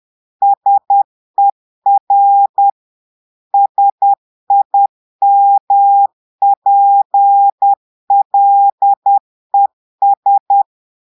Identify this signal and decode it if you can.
Morse code, heard as a keyed tone, about 10 wpm, 800 Hz. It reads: SER SIMPLES
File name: portugais texte en clair 2 vitesse 10 mots minutes